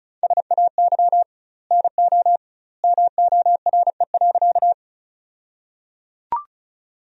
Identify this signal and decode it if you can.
Morse code, heard as a keyed tone, about 34 wpm, 700 Hz. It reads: SAY NO MORE.
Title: 2x Word-Spacing - Sentences from Top 200 Words - 35wpm